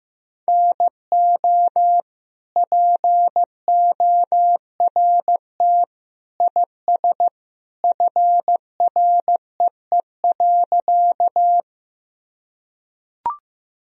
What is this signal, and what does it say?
Morse code, heard as a keyed tone, about 15 wpm, 700 Hz. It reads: NO PORT IS FREE.
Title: Sentences from Top 1000 Words - Review Part 1 - 15wpm